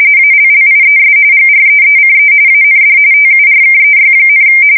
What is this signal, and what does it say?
RTTY, about 45.45 baud. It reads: ALLIES GREECE AND TURKEY,
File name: fsk170 / battleshipnj